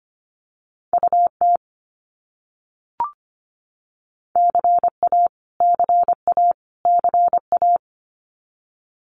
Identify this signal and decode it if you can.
Morse code, heard as a keyed tone, about 25 wpm, 700 Hz. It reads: UT CA CA CA
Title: US States Repeated 3x 25wpm